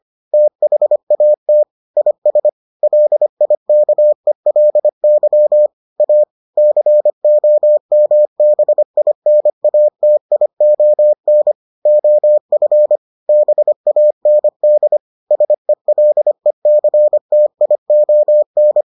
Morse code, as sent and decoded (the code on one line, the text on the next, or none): - .... .- - / .. ... / .-.. .. -.- . .-.. -.-- / .- / -.-. --- -- -... .. -. .- - .. --- -. / --- ..-. / -... .- -. -.. / ... .-.. -.-. - .. --- -.
THAT IS LIKELY A COMBINATION OF BAND SELECTION